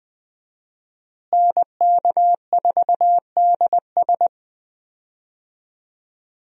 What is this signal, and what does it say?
Morse code, heard as a keyed tone, about 20 wpm, 700 Hz. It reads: NK4DS
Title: USA Call Signs - 5 Characters - Repeated 2x - 20wpm